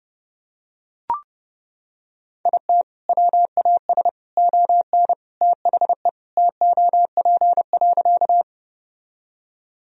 Morse code, heard as a keyed tone, about 32 wpm, 700 Hz. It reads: IT WAS ON THE TOP.
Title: Sentences from Top 1000 Words - Review Part 2 - 30wpm